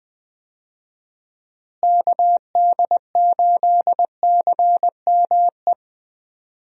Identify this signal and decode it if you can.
Morse code, heard as a keyed tone, about 20 wpm, 700 Hz. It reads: KD8CME
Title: USA Call Signs - 6 Characters - Repeated 3x - 20wpm